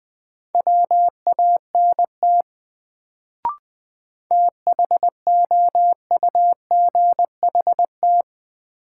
Morse code, heard as a keyed tone, about 20 wpm, 700 Hz. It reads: WANT THOUGHT
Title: Sets of 2 Words - New Word Drill - 20wpm